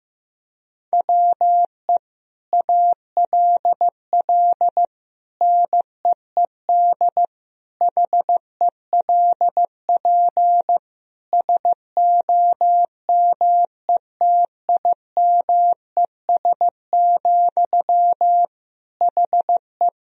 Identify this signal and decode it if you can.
Morse code, heard as a keyed tone, about 15 wpm, 700 Hz. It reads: WE ALL NEED HELP SOMETIMES, HE
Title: Sentences from Top 1000 Words - Review Part 3 - 15wpm